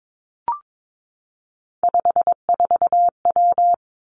Morse code, heard as a keyed tone, about 22 wpm, 700 Hz. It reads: 54W
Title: Three Challenging Letter-Number Combinations 22wpm